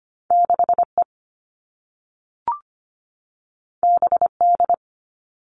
Morse code, heard as a keyed tone, about 25 wpm, 700 Hz. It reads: 6 E B D
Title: All Cut Numbers - 2-Digit 25wpm